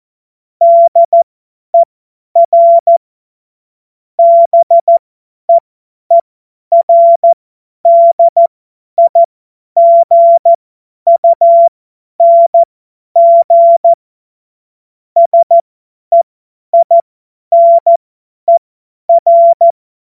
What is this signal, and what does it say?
Morse code, heard as a keyed tone, about 14 wpm, spaced as for 10 wpm, 680 Hz. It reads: DER BEERDIGUNG SEINER
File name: Morsekurs / Witze / Witz 18. Januar 2022, 50 F70